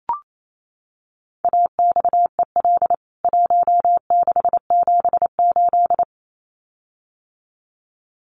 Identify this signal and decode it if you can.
Morse code, heard as a keyed tone, about 28 wpm, 700 Hz. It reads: AXEL 1678
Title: CWT Exchange 28wpm